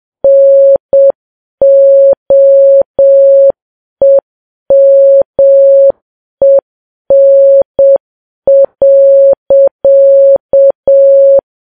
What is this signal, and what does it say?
Morse code, heard as a keyed tone, about 7 wpm, 550 Hz. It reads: NOEMEN.
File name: Files Web tls / Morse on9fv / morse7kps